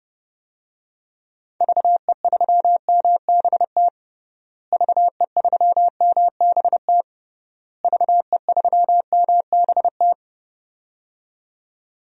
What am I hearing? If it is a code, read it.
Morse code, heard as a keyed tone, about 30 wpm, 700 Hz. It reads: VE3MBT VE3MBT VE3MBT
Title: DXCC-100 Call Signs - 6 Characters - Repeated 3x - 30wpm